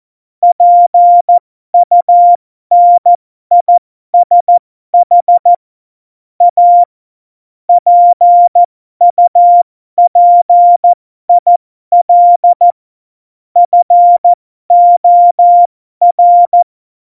Morse code, heard as a keyed tone, about 14 wpm, spaced as for 12 wpm, 700 Hz. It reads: PUNISH A PUPIL FOR
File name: Morsekurs / Witze